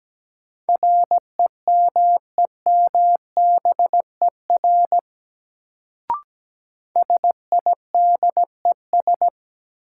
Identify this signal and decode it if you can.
Morse code, heard as a keyed tone, about 17 wpm, 700 Hz. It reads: REMEMBER SIDES
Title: Sets of 4 Words - New Word Drill - 17wpm